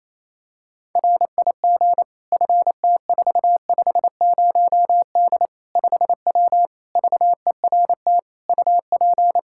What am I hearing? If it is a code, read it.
Morse code, heard as a keyed tone, about 28 wpm, 700 Hz. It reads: RIG FT450D 5W VERT UP
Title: Practice QSO - Exchange 2 - Info - 28wpm